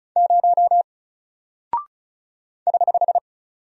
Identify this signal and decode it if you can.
Morse code, heard as a keyed tone, about 36 wpm, 700 Hz. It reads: 0 <HH>
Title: Single Character (Letter:Number:Punctuation:Prosign) 35wpm